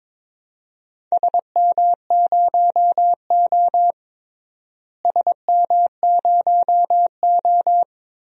Morse code, heard as a keyed tone, about 22 wpm, 700 Hz. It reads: SM0O SM0O
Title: DXCC-100 Call Signs - 4 Characters - Repeated 2x - 22wpm